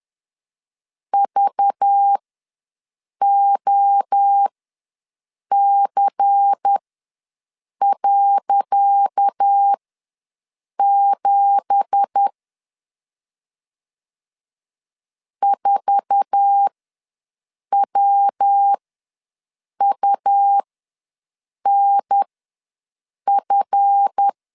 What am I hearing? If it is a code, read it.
Morse code, heard as a keyed tone, about 10 wpm, spaced as for 6 wpm, 800 Hz. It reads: VOC.7 4WUNF
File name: LEZIONI / 10 SIGNS / 10C DOT ws 10wpm 4213